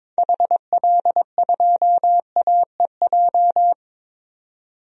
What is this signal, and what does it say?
Morse code, heard as a keyed tone, about 22 wpm, 700 Hz. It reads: HL2AEJ
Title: DXCC-100 Call Signs - 6 Characters - 22wpm